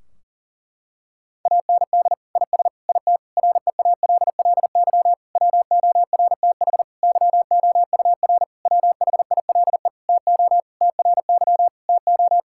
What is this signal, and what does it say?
Morse code, heard as a keyed tone, about 40 wpm, 700 Hz. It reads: AND IS IT REALLY WORTH YOUR WHILE TO TRY TO
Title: Sentences from Top 1600 Words 40wpm